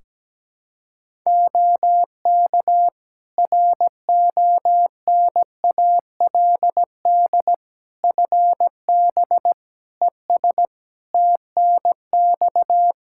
Morse code, heard as a keyed tone, about 17 wpm, 700 Hz. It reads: OK RONALD FB ES TNX
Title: Practice QSO - Exchange 2 - Info - 17wpm